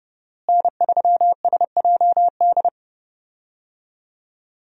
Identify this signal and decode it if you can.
Morse code, heard as a keyed tone, about 30 wpm, 700 Hz. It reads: N3SJD
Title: USA Call Signs - 5 Characters - Repeated 3x - 30wpm